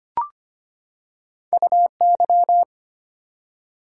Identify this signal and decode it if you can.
Morse code, heard as a keyed tone, about 25 wpm, 700 Hz. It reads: UY
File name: Two Letter-Number Combinations - 25wpm